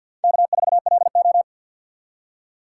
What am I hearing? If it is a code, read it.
Morse code, heard as a keyed tone, about 46 wpm, 700 Hz. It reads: K4LO
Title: USA Call Signs - 3x Character Spacing - 50wpm